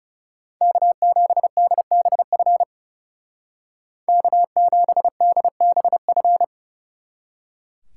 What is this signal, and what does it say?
Morse code, heard as a keyed tone, about 37 wpm, 700 Hz. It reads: K7DBF K7DBF
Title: Speed Racing - USA 1x3 Call Signs 30wpm